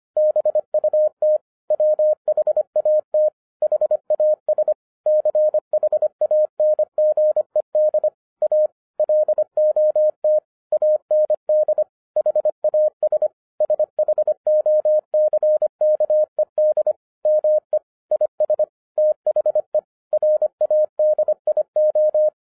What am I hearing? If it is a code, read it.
Morse code, heard as a keyed tone, about 25 wpm, 610 Hz. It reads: BUT WHAT HAS CHANGED A LOT AND HAS SHOCKED ME IS THE RADIO